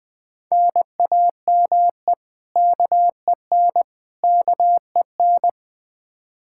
Morse code, heard as a keyed tone, about 20 wpm, 700 Hz. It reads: NAME KEN KEN
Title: Practice QSO - Exchange 1 - Beginning - 20wpm